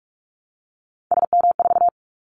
Morse code, heard as a keyed tone, about 44 wpm, 700 Hz. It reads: SM4
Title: Callsign Prefixes for Top 25 DXCC - Repeated 3x - 45wpm